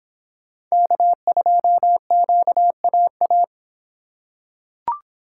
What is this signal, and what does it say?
Morse code, heard as a keyed tone, about 23 wpm, 700 Hz. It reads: K2QAA